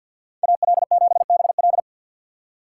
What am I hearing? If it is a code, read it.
Morse code, heard as a keyed tone, about 52 wpm, 700 Hz. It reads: AL7BL
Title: USA Call Signs - 5 Characters - 4x Character Spacing - 50wpm